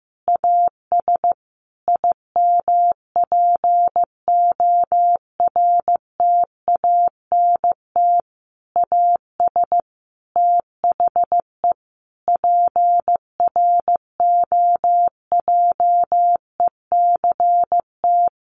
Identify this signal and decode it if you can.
Morse code, heard as a keyed tone, about 15 wpm, 700 Hz. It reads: AS IMPORTANT AS THE PROJECT
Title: Sentences from Top 1400 Words 15wpm